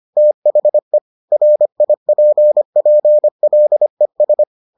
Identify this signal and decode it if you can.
Morse code, heard as a keyed tone, about 25 wpm, 600 Hz. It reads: THE RIPPLES